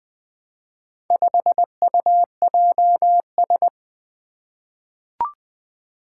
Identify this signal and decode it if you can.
Morse code, heard as a keyed tone, about 20 wpm, 700 Hz. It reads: 5UJS
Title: Four Challenging Letter-Number Combinations 20wpm